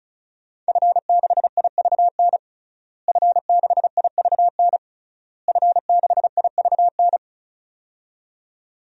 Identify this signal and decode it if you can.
Morse code, heard as a keyed tone, about 34 wpm, 700 Hz. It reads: F6IVN F6IVN F6IVN
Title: DXCC-100 Call Signs - 5 Characters - Repeated 3x - 35wpm